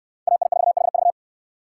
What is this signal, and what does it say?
Morse code, heard as a keyed tone, about 61 wpm, 700 Hz. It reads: AI4HV